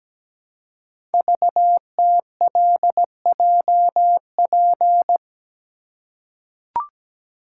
Morse code, heard as a keyed tone, about 17 wpm, 700 Hz. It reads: VTLJP
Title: Code Groups - 5 Letters - 17wpm